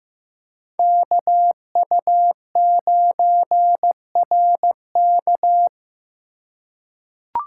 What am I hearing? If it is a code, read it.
Morse code, heard as a keyed tone, about 15 wpm, 700 Hz. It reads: KU9RK
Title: K1USN SST QSO Practice - 15wpm